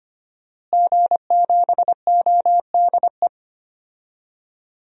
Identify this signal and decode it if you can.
Morse code, heard as a keyed tone, about 25 wpm, 700 Hz. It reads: G7ODE
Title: DXCC-100 Call Signs - 5 Characters - Repeated 2x - 25wpm